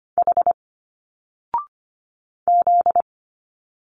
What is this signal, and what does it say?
Morse code, heard as a keyed tone, about 25 wpm, 700 Hz. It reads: H Z
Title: Single Letters 25wpm